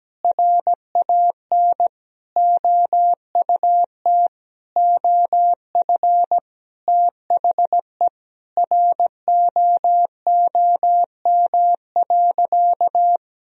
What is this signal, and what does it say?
Morse code, heard as a keyed tone, about 17 wpm, 700 Hz. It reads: RAN OUT OF THE ROOM.
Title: Sentences from Top 1000 Words - Review Part 3 - 17wpm